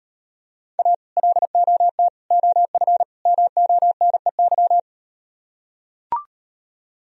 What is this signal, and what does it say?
Morse code, heard as a keyed tone, about 37 wpm, 700 Hz. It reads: A LOT OF MONEY